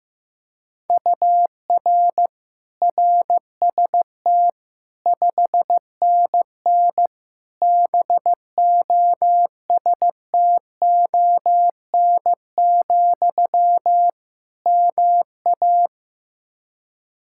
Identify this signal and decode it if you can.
Morse code, heard as a keyed tone, about 15 wpm, 700 Hz. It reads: UR RST 5NN BOSTON, MA
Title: Practice QSO - Exchange 1 - Beginning - 15wpm